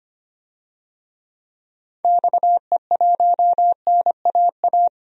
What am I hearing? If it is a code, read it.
Morse code, heard as a keyed tone, about 25 wpm, 700 Hz. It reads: XE1NAA
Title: DXCC-100 Call Signs - 6 Characters - 25wpm